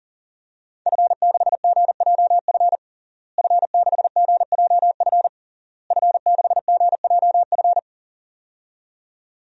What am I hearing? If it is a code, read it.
Morse code, heard as a keyed tone, about 40 wpm, 700 Hz. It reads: F6GJF F6GJF F6GJF
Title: DXCC-100 Call Signs - 5 Characters - Repeated 3x - 40wpm